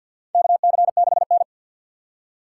Morse code, heard as a keyed tone, about 47 wpm, 700 Hz. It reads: KC6N
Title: USA Call Signs - 4 Characters - 3x Character Spacing - 50wpm